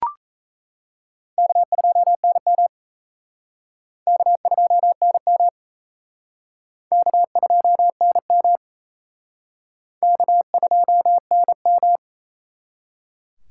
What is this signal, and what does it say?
Morse code, heard as a keyed tone, about 36 wpm, 700 Hz. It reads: K2NM K2NM K2NM K2NM